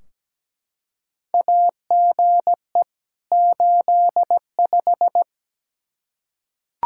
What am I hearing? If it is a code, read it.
Morse code, heard as a keyed tone, about 17 wpm, 700 Hz. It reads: AGE 85
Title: Practice QSO - Exchange 3 - More Info - 17wpm